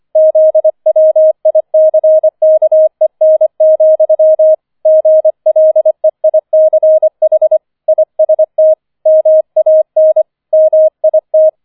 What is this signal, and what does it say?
Morse code, heard as a keyed tone, about 25 wpm, 630 Hz. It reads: ZWICKEN, GLEICH IST MAN MIT
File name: 005 Fünfter Streich 25wpm